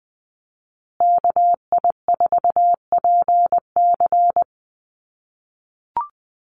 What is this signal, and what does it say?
Morse code, heard as a keyed tone, about 20 wpm, 700 Hz. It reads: KI4PC
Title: USA Call Signs Repeated 3x 20wpm